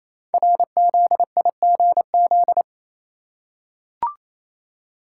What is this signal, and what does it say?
Morse code, heard as a keyed tone, about 27 wpm, 700 Hz. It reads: RZIGZ
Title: Code Groups - 5 Letters - 28wpm